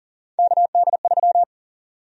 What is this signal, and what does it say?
Morse code, heard as a keyed tone, about 39 wpm, 700 Hz. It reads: KD3
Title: USA Call Sign Prefixes - 40wpm